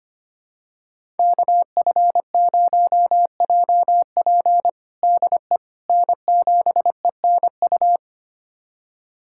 Morse code, heard as a keyed tone, about 25 wpm, 700 Hz. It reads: KF0JP DE N7ENU
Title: USA Callsign DE Callsign - 5 Characters - 25wpm